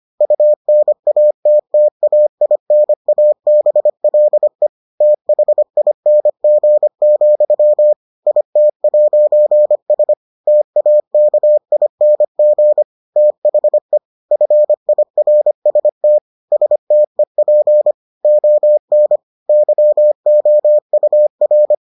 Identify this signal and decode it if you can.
Morse code, heard as a keyed tone, about 25 wpm, 600 Hz. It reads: UNATTAINABLE THING, IT'S TAKING THE FIRST STEP ON YOUR